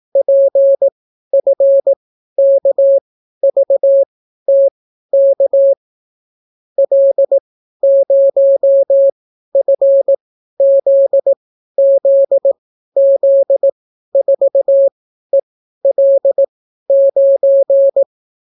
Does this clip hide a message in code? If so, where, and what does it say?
Yes: Morse code from 0.2 to 18.0 s, heard as a keyed tone, about 18 wpm, spaced as for 12 wpm, 550 Hz. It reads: PFKVTK L0FZZZ4EL9